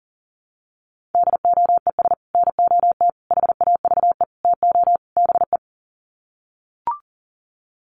Morse code, heard as a keyed tone, about 40 wpm, 700 Hz. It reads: DOES NOT HAVE TO BE
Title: Sets of 5 Words 40wpm